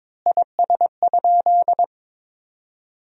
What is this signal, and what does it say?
Morse code, heard as a keyed tone, about 22 wpm, 700 Hz. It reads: IS?
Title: 2x Word-Spacing - Sentences from Top 100 Words - 22wpm